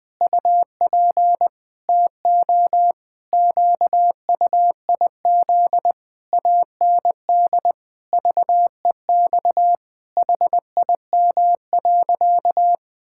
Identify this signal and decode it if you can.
Morse code, heard as a keyed tone, about 20 wpm, 700 Hz. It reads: UP TO QUIZ AND VEX HIM.
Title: Pangram Challenge v2 - 20wpm